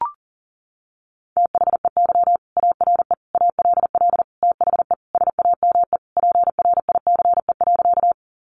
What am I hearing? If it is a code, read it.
Morse code, heard as a keyed tone, about 40 wpm, 700 Hz. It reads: THEY ARE ALL THE SAME PRICE.
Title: Sentences from Top 600 Words 40wpm